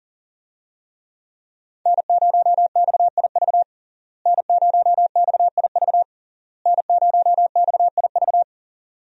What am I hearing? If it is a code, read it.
Morse code, heard as a keyed tone, about 40 wpm, 700 Hz. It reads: N0XIV N0XIV N0XIV
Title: USA Call Signs - 5 Characters - Repeated 3x - 40wpm